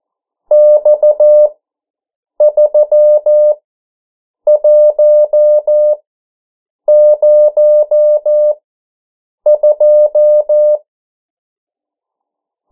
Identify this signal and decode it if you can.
Morse code, heard as a keyed tone, about 14 wpm, spaced as for 7 wpm, 600 Hz. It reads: X3102
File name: morse mp3 / 09 urok 8